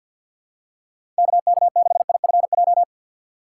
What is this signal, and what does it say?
Morse code, heard as a keyed tone, about 50 wpm, 690 Hz. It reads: KK6IFJ
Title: USA Call Signs - 6 Characters - 3x Character Spacing - 50wpm